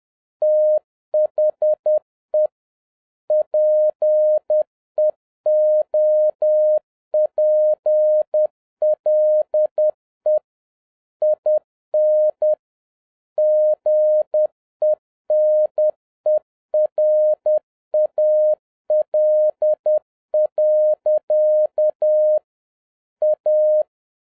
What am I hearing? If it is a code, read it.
Morse code, heard as a keyed tone, about 10 wpm, 610 Hz. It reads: THE PEOPLE IN GENERAL. A